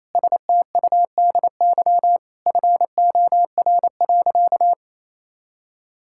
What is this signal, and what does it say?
Morse code, heard as a keyed tone, about 28 wpm, 700 Hz. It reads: STUDY FOR.
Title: Sentences from Top 1000 Words - Review Part 5 - 28wpm